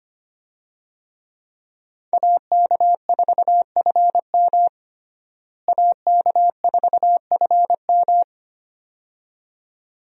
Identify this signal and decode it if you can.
Morse code, heard as a keyed tone, about 25 wpm, 700 Hz. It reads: AK4FM AK4FM
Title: USA Call Signs - 5 Characters - Repeated 2x - 25wpm